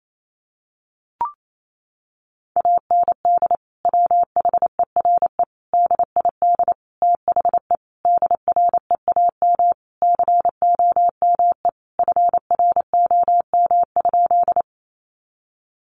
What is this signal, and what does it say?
Morse code, heard as a keyed tone, about 28 wpm, 700 Hz. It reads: AND WHERE DID THE DREAM COME FROM?
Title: Sentences from Top 1500 Words 28wpm